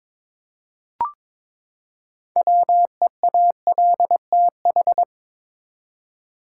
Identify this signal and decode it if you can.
Morse code, heard as a keyed tone, about 22 wpm, 700 Hz. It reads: WEALTH
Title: Six Letter Words - 22wpm